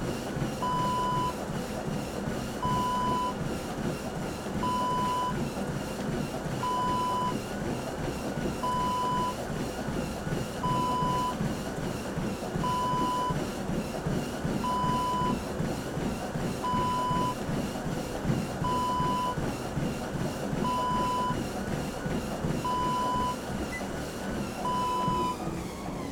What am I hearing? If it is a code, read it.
There is no code.